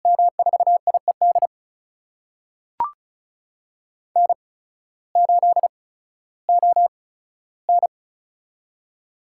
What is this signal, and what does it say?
Morse code, heard as a keyed tone, about 34 wpm, spaced as for 16 wpm, 700 Hz. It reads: M4IED N8ON
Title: USA Call Signs - 8x Character Spacing - 35wpm